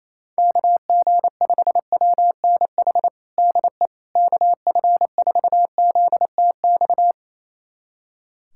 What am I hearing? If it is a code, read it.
Morse code, heard as a keyed tone, about 28 wpm, 700 Hz. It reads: KG5WNH DE KF4ZTX
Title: USA Callsign DE Callsign 28wpm